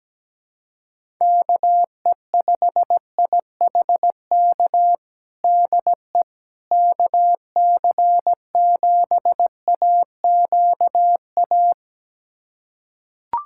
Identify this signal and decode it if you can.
Morse code, heard as a keyed tone, about 17 wpm, 700 Hz. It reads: KE5IHK DE KC7AQA
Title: USA Callsign DE Callsign - 6 Characters - 17wpm